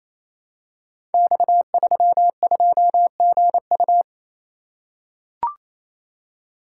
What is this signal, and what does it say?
Morse code, heard as a keyed tone, about 28 wpm, 700 Hz. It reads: X32GU
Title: Code Groups - 5 Letter:Numbers - 28wpm